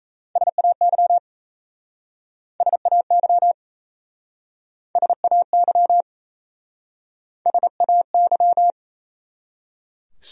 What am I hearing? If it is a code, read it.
Morse code, heard as a keyed tone, about 37 wpm, 700 Hz. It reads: SAY SAY SAY SAY